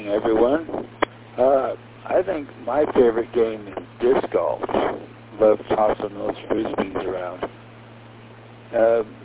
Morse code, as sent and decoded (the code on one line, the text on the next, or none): none